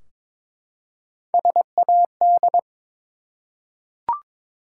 Morse code, heard as a keyed tone, about 23 wpm, 700 Hz. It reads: SAD
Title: Three Letter Words - 22wpm